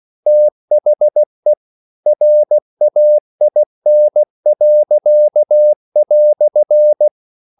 Morse code, heard as a keyed tone, about 16 wpm, 600 Hz. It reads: THE RAIN."
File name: CW / Practice / EnglishQuotations0116